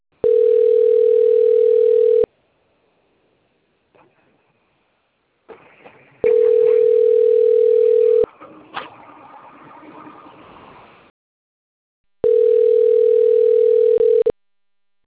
Ringback tone (North America), 440 + 480 Hz.
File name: out- - - - - .21228